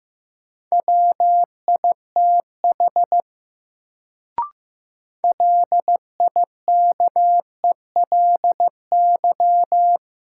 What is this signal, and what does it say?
Morse code, heard as a keyed tone, about 15 wpm, 700 Hz. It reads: WITH LIKELY
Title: Sets of 2 Words - New Word Drill - 15wpm